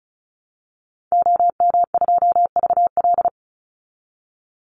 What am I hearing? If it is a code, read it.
Morse code, heard as a keyed tone, about 34 wpm, 700 Hz. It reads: OM2VL
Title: Call Signs Repeated 3x - 35wpm